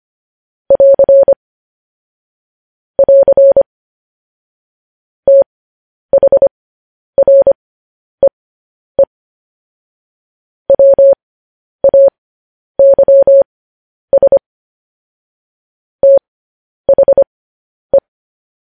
Morse code, heard as a keyed tone, about 25 wpm, spaced as for 10 wpm, 550 Hz. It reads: <AR> <AR> THREE WAYS THE
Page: news-headlines-in-morse-code-at-10-wpm